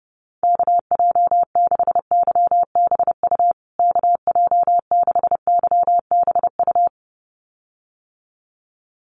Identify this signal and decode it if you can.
Morse code, heard as a keyed tone, about 30 wpm, 700 Hz. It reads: KJ6YBU KJ6YBU
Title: USA Call Signs Repeated 2x 30wpm